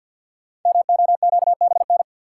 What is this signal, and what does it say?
Morse code, heard as a keyed tone, about 46 wpm, 700 Hz. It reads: MOQBN
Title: Code Groups - 5 Letters - 50wpm